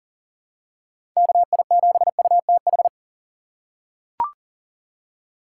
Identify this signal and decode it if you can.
Morse code, heard as a keyed tone, about 40 wpm, 700 Hz. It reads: KI7UTH
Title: USA Call Signs - 6 Characters - 6x Character Spacing - 40wpm